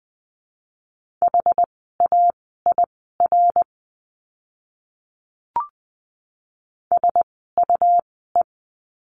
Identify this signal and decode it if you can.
Morse code, heard as a keyed tone, about 20 wpm, spaced as for 15 wpm, 700 Hz. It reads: HAIR SUE
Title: 101 Lesson 17f - Letter u - Words 20wpm